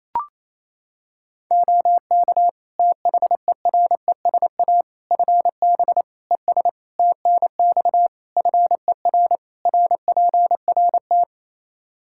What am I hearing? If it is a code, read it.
Morse code, heard as a keyed tone, about 28 wpm, 700 Hz. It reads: OK THERESA FB ES TNX FER RPRT
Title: Practice QSO - Exchange 2 - Info - 28wpm